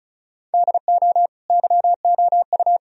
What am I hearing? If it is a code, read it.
Morse code, heard as a keyed tone, about 34 wpm, 700 Hz. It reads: DO YOU
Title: Sentences from Top 1000 Words - Review Part 1 - 35wpm